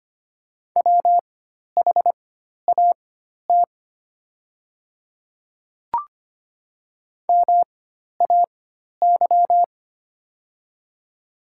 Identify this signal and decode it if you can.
Morse code, heard as a keyed tone, about 25 wpm, spaced as for 12 wpm, 700 Hz. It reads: WHAT MAY
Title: 4x Character Spacing - Top 200 Words - 25wpm